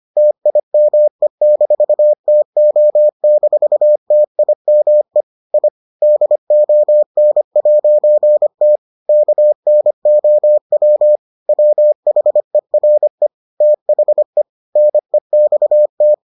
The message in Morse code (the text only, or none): TIME-TO-TIME I DON'T KNOW WHERE THE NEXT